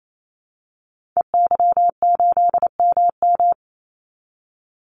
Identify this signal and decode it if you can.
Morse code, heard as a keyed tone, about 28 wpm, 700 Hz. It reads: EY8MM
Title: Call Signs Repeated 3x - 28wpm